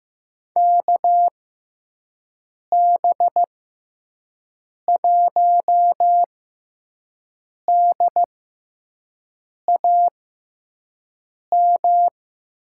Morse code, heard as a keyed tone, about 15 wpm, spaced as for 5 wpm, 700 Hz. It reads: KB1DAM
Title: USA Call Signs - 6 Characters - 6x Character Spacing - 15wpm